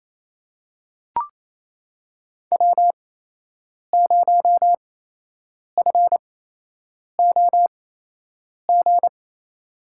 Morse code, heard as a keyed tone, about 27 wpm, spaced as for 8 wpm, 700 Hz. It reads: W0FOG
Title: USA Call Signs - 5 Characters - 8x Character Spacing - 28wpm